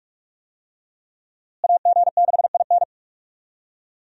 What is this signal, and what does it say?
Morse code, heard as a keyed tone, about 46 wpm, 700 Hz. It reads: AG6IN
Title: USA Call Signs - 5 Characters - Repeated 3x - 45wpm